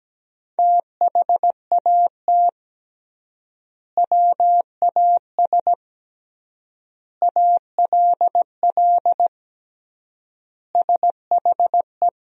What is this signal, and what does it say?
Morse code, heard as a keyed tone, about 17 wpm, 700 Hz. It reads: THAT WAS ALL SHE
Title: 3x Word-Spacing - Sentences from Top 100 Words - 17wpm